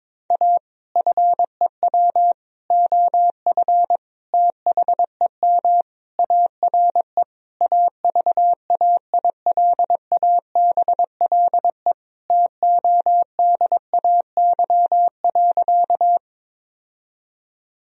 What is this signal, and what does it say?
Morse code, heard as a keyed tone, about 22 wpm, 700 Hz. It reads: A FEW OF THEM ARE AVAILABLE TODAY.
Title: Sentences from Top 1000 Words - Review Part 3 - 22wpm